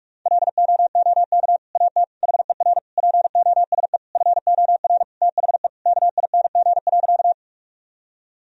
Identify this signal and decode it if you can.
Morse code, heard as a keyed tone, about 46 wpm, 700 Hz. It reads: LOOK AT HER POSE FOR THE KING.
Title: Sentences from Top 1000 Words - Review Part 2 - 45wpm